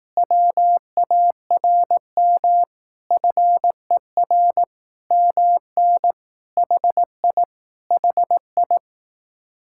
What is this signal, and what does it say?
Morse code, heard as a keyed tone, about 18 wpm, 700 Hz. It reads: WARM FER MN HI HI
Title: Speed-Racing Common QSO Phrases Encore - 15wpm